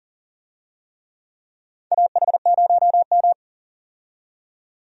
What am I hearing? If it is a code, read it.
Morse code, heard as a keyed tone, about 40 wpm, 700 Hz. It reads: AH0M
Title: USA Call Signs - 4 Characters - 40wpm